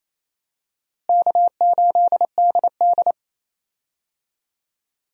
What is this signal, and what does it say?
Morse code, heard as a keyed tone, about 28 wpm, 700 Hz. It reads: K8DD
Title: USA Call Signs - 4 Characters - Repeated 3x - 28wpm